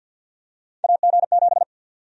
Morse code, heard as a keyed tone, about 46 wpm, 700 Hz. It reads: AG7
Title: USA Call Sign Prefixes - 50wpm